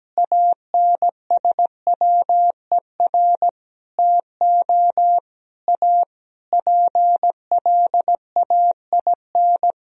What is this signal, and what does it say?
Morse code, heard as a keyed tone, about 17 wpm, 700 Hz. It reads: ANSWER TO A PLAIN
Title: Sentences from Top 1000 Words - Review Part 2 - 17wpm